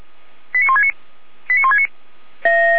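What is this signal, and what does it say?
Five-tone selective call (ZVEI), heard as quick runs of five tones, tones of about 70 ms. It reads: 89169 89169
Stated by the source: ZVEI five-tone call received on an SDR, logged by Feuerwehr Orlamünde.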